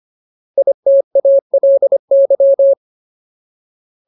Morse code, heard as a keyed tone, about 25 wpm, 550 Hz. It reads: ITALY
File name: Ultimate Word List / 25WPM Ultimate